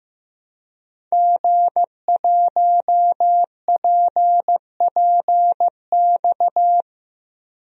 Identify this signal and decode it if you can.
Morse code, heard as a keyed tone, about 16 wpm, 700 Hz. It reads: G1PPX
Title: DXCC-100 Call Signs - 5 Characters - Repeated 2x - 15wpm